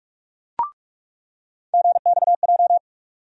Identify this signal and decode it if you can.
Morse code, heard as a keyed tone, about 41 wpm, 700 Hz. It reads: GXJ
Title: USA Call Sign Postfixes - 3 Letters 45wpm